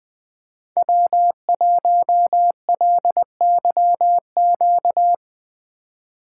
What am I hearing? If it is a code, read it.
Morse code, heard as a keyed tone, about 20 wpm, 700 Hz. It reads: W1LYQ
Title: USA Call Signs - 3x Character Spacing - 20wpm